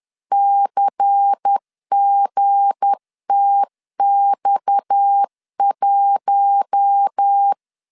Morse code, heard as a keyed tone, about 10 wpm, 800 Hz. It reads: CGTX1
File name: LEZIONI / 7 GF9Z / 7 GF9Z 10wpm 2611